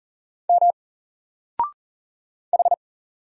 Morse code, heard as a keyed tone, about 40 wpm, 700 Hz. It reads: M H
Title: Single Character (Letter:Number:Punctuation:Prosign) 40wpm